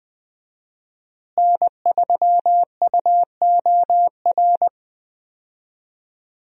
Morse code, heard as a keyed tone, about 20 wpm, 700 Hz. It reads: N3UOR
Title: USA Call Signs - 5 Characters - Repeated 2x - 20wpm